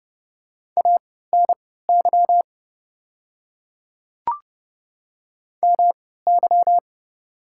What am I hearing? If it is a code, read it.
Morse code, heard as a keyed tone, about 30 wpm, spaced as for 17 wpm, 700 Hz. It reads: ANY MY
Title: 3x Character Spacing - Top 200 Words - 30wpm